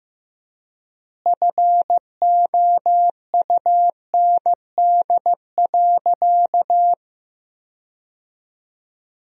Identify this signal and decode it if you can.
Morse code, heard as a keyed tone, about 15 wpm, 700 Hz. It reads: FOUND.
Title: 3x Word-Spacing - Sentences from Top 300 Words - 15wpm